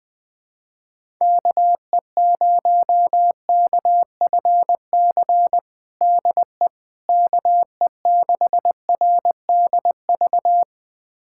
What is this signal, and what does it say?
Morse code, heard as a keyed tone, about 21 wpm, 700 Hz. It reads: KE0KFC DE KE6RDV
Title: USA Callsign DE Callsign - 6 Characters - 20wpm